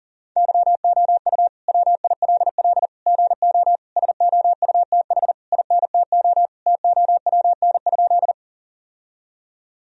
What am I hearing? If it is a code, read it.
Morse code, heard as a keyed tone, about 40 wpm, 700 Hz. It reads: YOU WILL GO SOUTH INTO TOWN?
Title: Sentences from Top 1000 Words - Review Part 2 - 40wpm